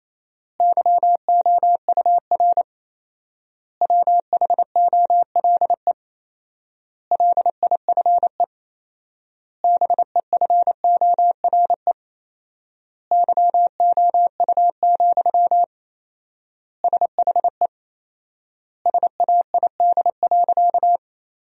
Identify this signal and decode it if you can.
Morse code, heard as a keyed tone, about 28 wpm, 700 Hz. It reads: YOUR WHOLE LIFE BEFORE YOU, SHE SAID.
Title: 4x Word-Spacing - Sentences from Top 400 Words - 28wpm